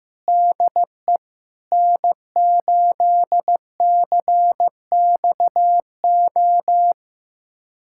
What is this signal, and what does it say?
Morse code, heard as a keyed tone, about 15 wpm, 700 Hz. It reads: DE N8CXO
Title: USA Callsign DE Callsign - 5 Characters - 15wpm